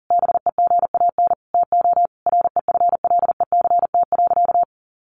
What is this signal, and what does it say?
Morse code, heard as a keyed tone, about 37 wpm, 700 Hz. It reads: BEGAN TO REFLECT.
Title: Sentences from Top 1500 Words 40wpm